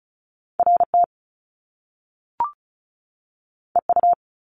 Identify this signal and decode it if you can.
Morse code, heard as a keyed tone, about 36 wpm, 700 Hz. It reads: RT EU
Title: Top Two Letter Combinations in Words 35wpm